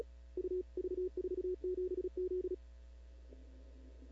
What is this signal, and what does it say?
Morse code, heard as a keyed tone, about 37 wpm, 370 Hz. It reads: U V47Z
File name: CW / 2005-k5tr arrl cw dx -15:00